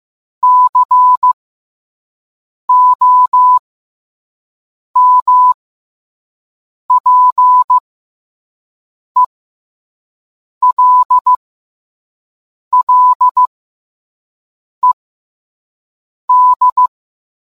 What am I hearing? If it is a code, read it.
Morse code, heard as a keyed tone, about 15 wpm, spaced as for 5 wpm, 1000 Hz. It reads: COMPELLED